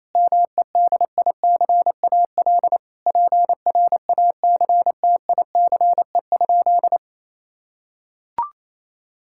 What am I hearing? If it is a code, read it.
Morse code, heard as a keyed tone, about 28 wpm, 700 Hz. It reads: MEDICAL PRACTICE?
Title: Sentences from Top 1000 Words - Review Part 5 - 28wpm